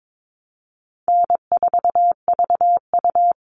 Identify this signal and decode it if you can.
Morse code, heard as a keyed tone, about 22 wpm, 700 Hz. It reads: N4VU
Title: All Cut Numbers - 4-Digit 22wpm